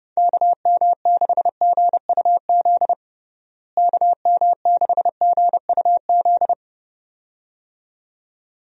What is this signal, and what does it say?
Morse code, heard as a keyed tone, about 30 wpm, 700 Hz. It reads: KM6GUZ KM6GUZ
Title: USA Call Signs - 6 Characters - Repeated 2x - 30wpm